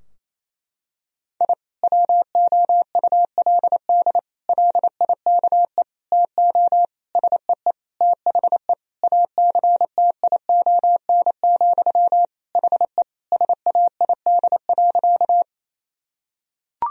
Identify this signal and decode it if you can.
Morse code, heard as a keyed tone, about 28 wpm, 700 Hz. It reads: I WOULD LIKE TO SEE THE ACTION, HE SAID.
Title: Sentences from Top 1000 Words - Review Part 4 - 28wpm